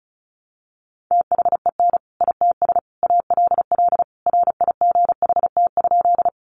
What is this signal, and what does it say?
Morse code, heard as a keyed tone, about 34 wpm, 700 Hz. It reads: THEN ITS ALL RIGHT?
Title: Sentences from Top 600 Words 35wpm